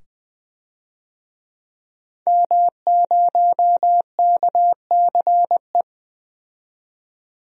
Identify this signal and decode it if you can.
Morse code, heard as a keyed tone, about 20 wpm, 700 Hz. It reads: M0KCE
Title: DXCC-100 Call Signs - 5 Characters - 20wpm